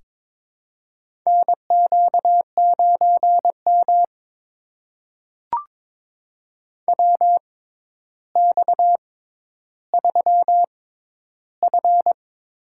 Morse code, heard as a keyed tone, about 22 wpm, spaced as for 11 wpm, 700 Hz. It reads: NQ9M WX3F
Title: USA Call Signs - 4 Characters - 6x Character Spacing - 22wpm